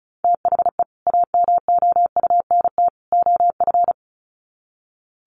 Morse code, heard as a keyed tone, about 34 wpm, 700 Hz. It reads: THE AMOUNT OF
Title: Sets of 3 Words 35wpm